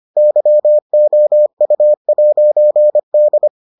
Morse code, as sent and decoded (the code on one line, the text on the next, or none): -.-- --- ..- .----. -..
YOU'D